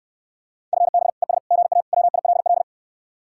Morse code, heard as a keyed tone, about 69 wpm, 700 Hz. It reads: FB ES GA WILL